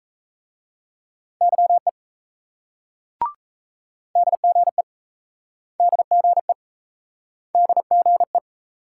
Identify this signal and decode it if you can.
Morse code, heard as a keyed tone, about 38 wpm, 700 Hz. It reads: YE DGE DGE DGE